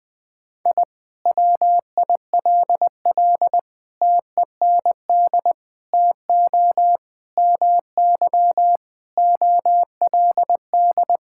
Morse code, heard as a keyed tone, about 21 wpm, 700 Hz. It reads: I WILL TEND TO MY OLD
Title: Sentences from Top 1000 Words - Review Part 5 - 20wpm